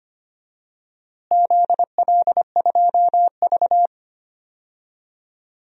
Morse code, heard as a keyed tone, about 25 wpm, 700 Hz. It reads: ZL2V
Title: DXCC-100 Call Signs - 4 Characters - Repeated 2x - 25wpm